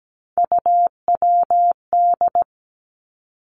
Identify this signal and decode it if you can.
Morse code, heard as a keyed tone, about 17 wpm, 700 Hz. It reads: UWD
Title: Three Challenging Letter-Number Combinations 17wpm